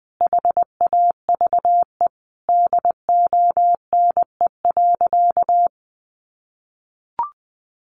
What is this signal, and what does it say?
Morse code, heard as a keyed tone, about 20 wpm, 700 Hz. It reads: HAVE DONE.
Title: Sentences from Top 500 Words 20wpm